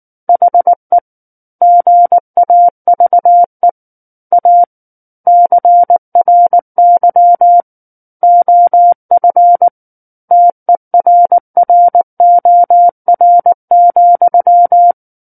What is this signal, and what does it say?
Morse code, heard as a keyed tone, about 19 wpm, 700 Hz. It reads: HE GAVE A CRY OF TERROR,
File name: Book One / 10-Chapter 10 - 19 WPM